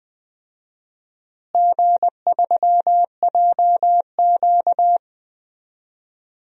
Morse code, heard as a keyed tone, about 20 wpm, 700 Hz. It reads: G3JQ
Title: DXCC-100 Call Signs - 4 Characters - Repeated 3x - 20wpm